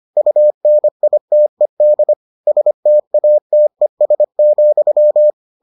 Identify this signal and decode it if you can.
Morse code, heard as a keyed tone, about 25 wpm, 600 Hz. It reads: UNITED STATES,